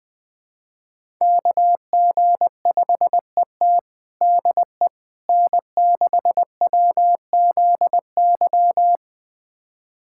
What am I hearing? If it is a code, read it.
Morse code, heard as a keyed tone, about 20 wpm, 700 Hz. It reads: KG5ET DE N6WZY
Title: USA Callsign DE Callsign - 5 Characters - 20wpm